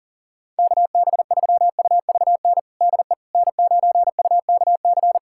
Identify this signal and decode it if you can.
Morse code, heard as a keyed tone, about 37 wpm, 700 Hz. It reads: KB3UVN DE N9UKC
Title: USA Callsign DE Callsign 40wpm